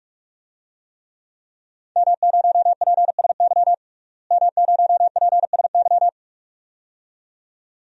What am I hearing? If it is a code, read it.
Morse code, heard as a keyed tone, about 46 wpm, 700 Hz. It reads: M0PSY M0PSY
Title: DXCC-100 Call Signs - 5 Characters - Repeated 2x - 45wpm